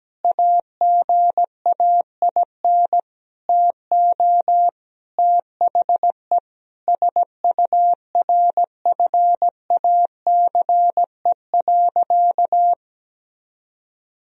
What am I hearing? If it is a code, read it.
Morse code, heard as a keyed tone, about 17 wpm, 700 Hz. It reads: AGAIN TO THE SURFACE.
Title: Sentences from Top 1000 Words - Review Part 3 - 17wpm